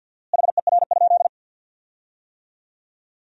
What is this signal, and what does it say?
Morse code, heard as a keyed tone, about 52 wpm, 700 Hz. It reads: HER?
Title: 4x Word-Spacing - Sentences from Top 100 Words - 50wpm